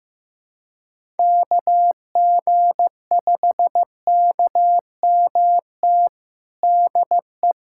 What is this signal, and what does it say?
Morse code, heard as a keyed tone, about 16 wpm, 700 Hz. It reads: KG5KMT DE
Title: USA Callsign DE Callsign - 6 Characters - 15wpm